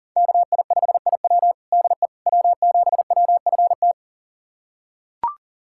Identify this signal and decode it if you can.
Morse code, heard as a keyed tone, about 40 wpm, 700 Hz. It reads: KI5IW DE W7WFT
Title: USA Callsign DE Callsign 40wpm